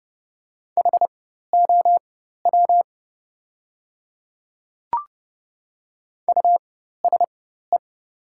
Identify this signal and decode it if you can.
Morse code, heard as a keyed tone, about 30 wpm, spaced as for 14 wpm, 700 Hz. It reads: HOW USE
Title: 4x Character Spacing - Top 100 Words - 30wpm